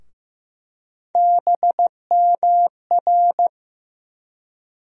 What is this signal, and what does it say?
Morse code, heard as a keyed tone, about 15 wpm, 700 Hz. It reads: BMR
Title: USA Call Sign Postfixes - 3 Letters 15wpm